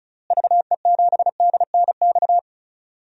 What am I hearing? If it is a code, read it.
Morse code, heard as a keyed tone, about 34 wpm, 700 Hz. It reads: VE7DNX
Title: DXCC-100 Call Signs - 6 Characters - Repeated 2x - 35wpm